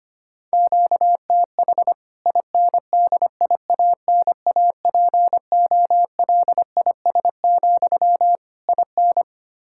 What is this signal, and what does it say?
Morse code, heard as a keyed tone, about 25 wpm, 700 Hz. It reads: QTH INDIANAPOLIS, IN
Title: K1USN SST QSO Practice 25wpm